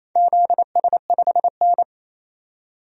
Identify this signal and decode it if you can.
Morse code, heard as a keyed tone, about 27 wpm, 690 Hz. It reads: ZS5N
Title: DXCC-100 Call Signs - 4 Characters - 28wpm